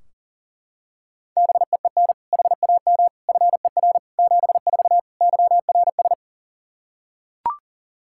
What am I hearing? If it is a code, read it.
Morse code, heard as a keyed tone, about 40 wpm, 700 Hz. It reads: BEEN HAM FER 74 YRS
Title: Practice QSO - Exchange 3 - More Info - 40wpm